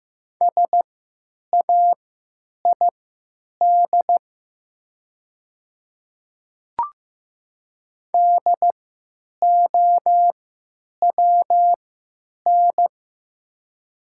Morse code, heard as a keyed tone, about 15 wpm, spaced as for 9 wpm, 690 Hz. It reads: SAID DOWN
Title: 3x Character Spacing - Top 100 Words - 15wpm